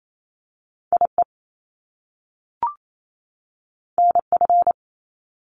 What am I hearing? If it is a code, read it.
Morse code, heard as a keyed tone, about 28 wpm, 700 Hz. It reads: IE NF
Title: Top Two Letter Combinations in Words 28wpm